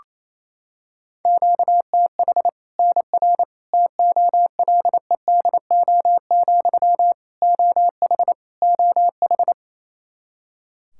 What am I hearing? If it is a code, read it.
Morse code, heard as a keyed tone, about 28 wpm, 700 Hz. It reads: QTH NR TOLEDO, OH OH
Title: Practice QSO - Exchange 1 - Beginning - 28wpm